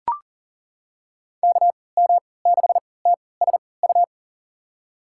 Morse code, heard as a keyed tone, about 38 wpm, spaced as for 22 wpm, 700 Hz. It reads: KM6TSU
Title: USA Call Signs - 6 Characters - 3x Character Spacing - 40wpm